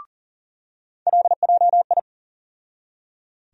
Morse code, heard as a keyed tone, about 40 wpm, 700 Hz. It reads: LJI
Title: USA Call Sign Postfixes - 3 Letters 40wpm